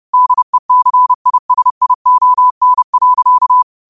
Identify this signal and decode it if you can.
Morse code, heard as a keyed tone, about 30 wpm, 1000 Hz. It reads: DECISION.